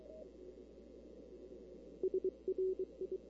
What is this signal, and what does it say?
Morse code, heard as a keyed tone, about 24 wpm, 380 Hz. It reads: SRI
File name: CW / 2005-k5tr arrl cw dx -16:00